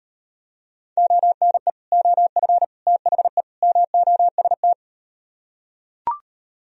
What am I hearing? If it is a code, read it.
Morse code, heard as a keyed tone, about 38 wpm, 700 Hz. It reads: ONE OF THE MOST